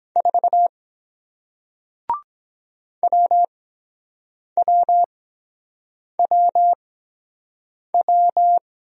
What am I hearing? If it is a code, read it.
Morse code, heard as a keyed tone, about 23 wpm, 700 Hz. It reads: 4 W W W W